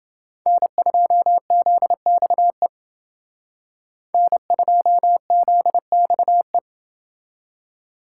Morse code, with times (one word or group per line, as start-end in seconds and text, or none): 0.5-2.7 N2ZXE
4.1-6.6 N2ZXE